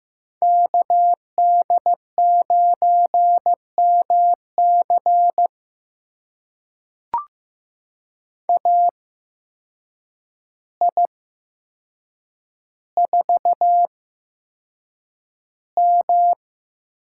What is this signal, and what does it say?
Morse code, heard as a keyed tone, about 15 wpm, spaced as for 7 wpm, 700 Hz. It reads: KD9MC AI4M
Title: USA Call Signs - 5 Characters - 8x Character Spacing - 15wpm